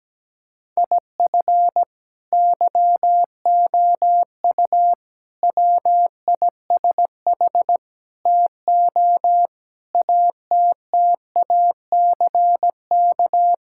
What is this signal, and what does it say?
Morse code, heard as a keyed tone, about 17 wpm, 700 Hz. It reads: IF YOU WISH TO ATTACK
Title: Sentences from Top 1600 Words 17wpm